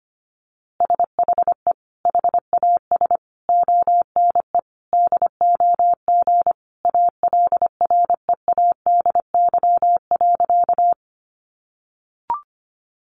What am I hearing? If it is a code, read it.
Morse code, heard as a keyed tone, about 25 wpm, 700 Hz. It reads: SHE HAS ONE DOG ALREADY.
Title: Easy Phrases 25wpm